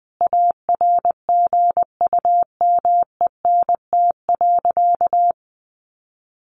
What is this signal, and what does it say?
Morse code, heard as a keyed tone, about 20 wpm, 700 Hz. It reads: ARGUMENT.
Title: Sentences from Top 1500 Words 20wpm